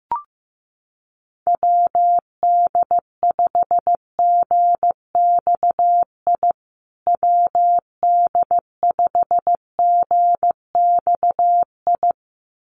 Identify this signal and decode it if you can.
Morse code, heard as a keyed tone, about 15 wpm, 700 Hz. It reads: WD5GXI WD5GXI
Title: USA Call Signs Repeated 2x 15wpm